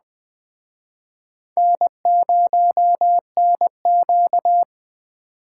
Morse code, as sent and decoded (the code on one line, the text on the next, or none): -. ----- -. --.-
N0NQ